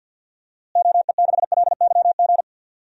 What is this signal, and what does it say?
Morse code, heard as a keyed tone, about 51 wpm, 700 Hz. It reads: OE6LYG
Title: DXCC-100 Call Signs - 6 Characters - Repeated 2x - 50wpm